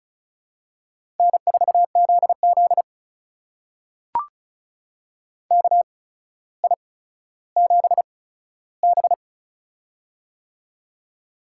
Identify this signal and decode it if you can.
Morse code, heard as a keyed tone, about 36 wpm, spaced as for 15 wpm, 700 Hz. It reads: N4ZZ KI7B
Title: USA Call Signs - 4 Characters - 8x Character Spacing - 35wpm